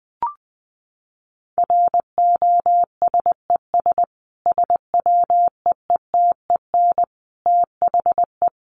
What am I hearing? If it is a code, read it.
Morse code, heard as a keyed tone, about 20 wpm, 700 Hz. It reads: ROSES SWEETEN THE
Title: A Sound Of Thunder 20wpm